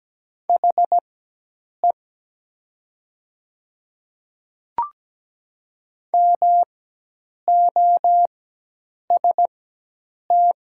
Morse code, heard as a keyed tone, about 17 wpm, spaced as for 8 wpm, 700 Hz. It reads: HE MOST
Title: 4x Character Spacing - Top 100 Words - 17wpm